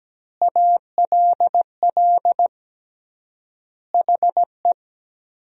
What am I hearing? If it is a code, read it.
Morse code, heard as a keyed tone, about 17 wpm, 700 Hz. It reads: ALL HE
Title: 3x Word-Spacing - Sentences from Top 400 Words - 17wpm